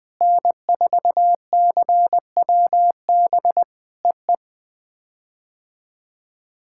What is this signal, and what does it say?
Morse code, heard as a keyed tone, about 21 wpm, 700 Hz. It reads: N4CWB EE
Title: POTA QSO Practice - 20wpm